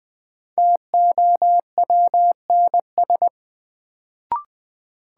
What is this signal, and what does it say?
Morse code, heard as a keyed tone, about 20 wpm, 700 Hz. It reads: TOWNS
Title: Sentences from Top 1000 Words - Review Part 6 - New Word Drill - 20wpm